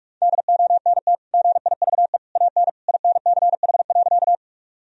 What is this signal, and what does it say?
Morse code, heard as a keyed tone, about 48 wpm, 700 Hz. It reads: DONT GIVE AN INCH.
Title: Sentences from Top 1000 Words - Review Part 3 - 45wpm